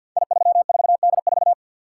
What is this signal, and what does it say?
Morse code, heard as a keyed tone, about 52 wpm, 700 Hz. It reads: I34D4
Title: Code Groups - 5 Letter:Numbers - 50wpm